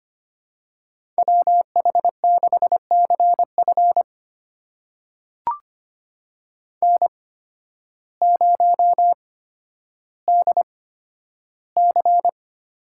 Morse code, heard as a keyed tone, about 25 wpm, spaced as for 12 wpm, 700 Hz. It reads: WH6CF N0DC
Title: USA Call Signs - 5 Characters - 8x Character Spacing - 25wpm